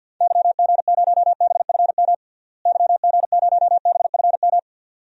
Morse code, heard as a keyed tone, about 50 wpm, 700 Hz. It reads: YG0BFM YG0BFM
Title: DXCC-100 Call Signs - 6 Characters - Repeated 2x - 50wpm